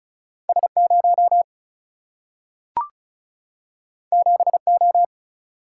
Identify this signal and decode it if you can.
Morse code, heard as a keyed tone, about 36 wpm, 700 Hz. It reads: S0 7O
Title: Single-Letter Single-Number 35wpm